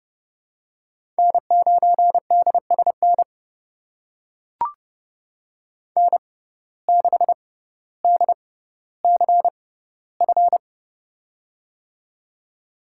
Morse code, heard as a keyed tone, about 30 wpm, spaced as for 15 wpm, 700 Hz. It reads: N9DSN N6DCF
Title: USA Call Signs - 5 Characters - 6x Character Spacing - 30wpm